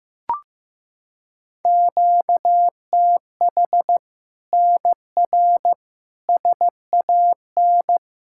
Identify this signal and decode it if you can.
Morse code, heard as a keyed tone, about 15 wpm, 700 Hz. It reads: QTH NR SAN
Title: Practice QSO - Exchange 1 - Beginning - 15wpm